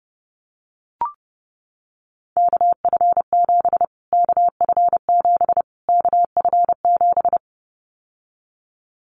Morse code, heard as a keyed tone, about 30 wpm, 700 Hz. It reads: KF7 KF7 KF7
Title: Callsign Prefixes for Top 25 DXCC - Repeated 3x - 30wpm